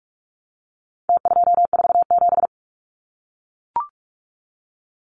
Morse code, heard as a keyed tone, about 44 wpm, 700 Hz. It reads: T247
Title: Numbers - Contest Style - 4-Digit 45wpm